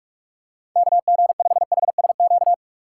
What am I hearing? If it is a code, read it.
Morse code, heard as a keyed tone, about 46 wpm, 700 Hz. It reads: KG5HSQ
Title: USA Call Signs - 6 Characters - 3x Character Spacing - 45wpm